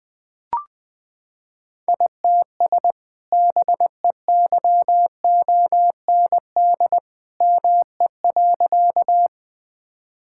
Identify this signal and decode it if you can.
Morse code, heard as a keyed tone, about 21 wpm, 700 Hz. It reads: ITS BEYOND ME.
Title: Sentences from Top 1000 Words - Review Part 6 - 20wpm